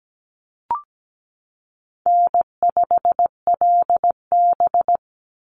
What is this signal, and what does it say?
Morse code, heard as a keyed tone, about 17 wpm, 700 Hz. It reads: N5LB
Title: Easy Call Signs Repeated 2x - 17wpm